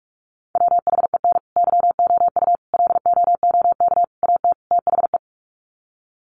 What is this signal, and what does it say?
Morse code, heard as a keyed tone, about 44 wpm, 700 Hz. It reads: WHEN YOU LOOK AT THE
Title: Sets of 5 Words 45wpm